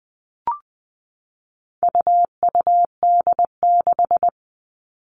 Morse code, heard as a keyed tone, about 20 wpm, 700 Hz. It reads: UUD6
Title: All Cut Numbers - 4-Digit 20wpm